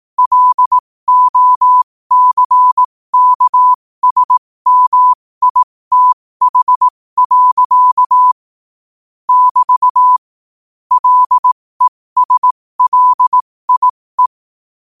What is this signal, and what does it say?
Morse code, heard as a keyed tone, about 18 wpm, spaced as for 15 wpm, 1000 Hz. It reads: LOCKSMITH. <BT> LESLIE
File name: qotdfiles / 15wpm